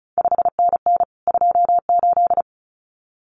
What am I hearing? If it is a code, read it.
Morse code, heard as a keyed tone, about 36 wpm, 700 Hz. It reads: 5NN 28
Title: CQ WW DX Contest Exchange Encore - 35wpm